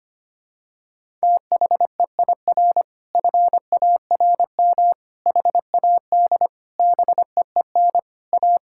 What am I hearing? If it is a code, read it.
Morse code, heard as a keyed tone, about 25 wpm, 700 Hz. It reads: THEIR FARM HAD BEEN A
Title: Sentences from Top 1000 Words - Review Part 6 - 25wpm